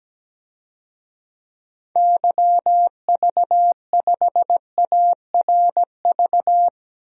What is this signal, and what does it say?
Morse code, heard as a keyed tone, about 17 wpm, 700 Hz. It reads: YV5ARV
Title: DXCC-100 Call Signs - 6 Characters - Repeated 3x - 17wpm